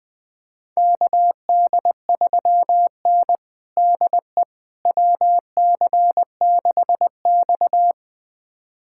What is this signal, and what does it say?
Morse code, heard as a keyed tone, about 20 wpm, 700 Hz. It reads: KD3N DE WC6X
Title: USA Callsign DE Callsign - 4 Characters - 20wpm